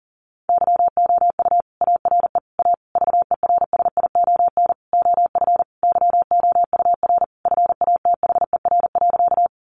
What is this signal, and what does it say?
Morse code, heard as a keyed tone, about 40 wpm, 700 Hz. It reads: YOU ARE A VERSION OF YOUR FATHER.
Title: Sentences from Top 1400 Words 40wpm